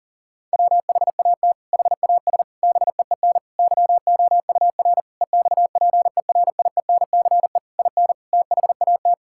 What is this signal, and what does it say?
Morse code, heard as a keyed tone, about 40 wpm, 700 Hz. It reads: WHAT HAS BEEN YOUR EXPERIENCE IN THAT
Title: Sentences from Top 1000 Words - Review Part 6 - 40wpm